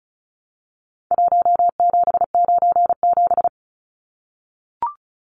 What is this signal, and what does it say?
Morse code, heard as a keyed tone, about 34 wpm, 700 Hz. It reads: 1797
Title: Numbers - Contest Style - 4-Digit 35wpm